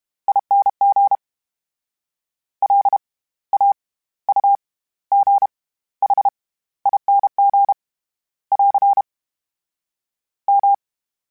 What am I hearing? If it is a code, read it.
Morse code, heard as a keyed tone, about 32 wpm, spaced as for 17 wpm, 800 Hz. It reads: ING LAUGHING <AR> M
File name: Special INGs / Special INGs-mp3s / SlowToFaster ing @30